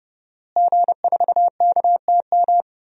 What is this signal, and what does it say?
Morse code, heard as a keyed tone, about 30 wpm, 700 Hz. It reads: G4KTM
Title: DXCC-100 Call Signs - 5 Characters - Repeated 3x - 30wpm